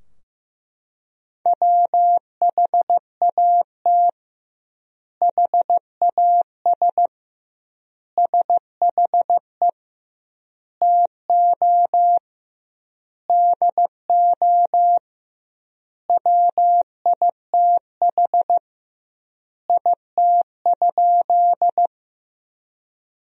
Morse code, heard as a keyed tone, about 15 wpm, 700 Hz. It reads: WHAT HAS SHE TO DO WITH IT?
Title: 2x Word-Spacing - Sentences from Top 100 Words - 15wpm